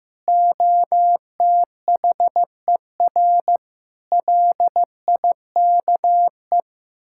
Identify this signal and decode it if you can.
Morse code, heard as a keyed tone, about 15 wpm, 700 Hz. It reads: OTHER LIKE
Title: Sentences from Top 1000 Words - Review Part 6 - 15wpm